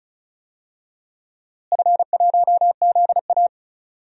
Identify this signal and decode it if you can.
Morse code, heard as a keyed tone, about 34 wpm, 700 Hz. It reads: F1ZA
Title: DXCC-100 Call Signs - 4 Characters - Repeated 3x - 35wpm